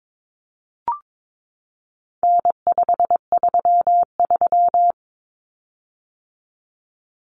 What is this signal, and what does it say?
Morse code, heard as a keyed tone, about 22 wpm, 700 Hz. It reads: N533
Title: Common Cut Numbers - 4-Digit 22wpm